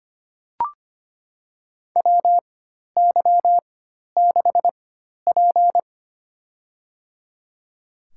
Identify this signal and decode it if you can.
Morse code, heard as a keyed tone, about 25 wpm, spaced as for 12 wpm, 700 Hz. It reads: WY6P
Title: USA Call Signs - 4 Characters - 4x Character Spacing - 25wpm